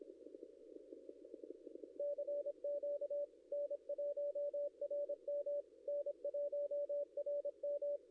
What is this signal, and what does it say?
Morse code, heard as a keyed tone, about 25 wpm, 560 Hz. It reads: CQ N1RM N1RM
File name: contests / ss-cw-k5tr-op-n5rz -0700